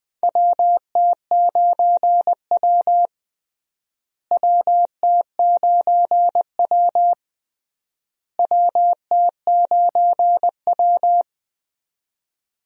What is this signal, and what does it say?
Morse code, heard as a keyed tone, about 20 wpm, 700 Hz. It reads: WT9W WT9W WT9W
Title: USA Call Signs - 4 Characters - Repeated 3x - 20wpm